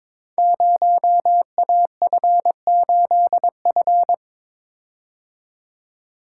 Morse code, heard as a keyed tone, about 22 wpm, 700 Hz. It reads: 0AF8F
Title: Code Groups - 5 Letter:Numbers - 22wpm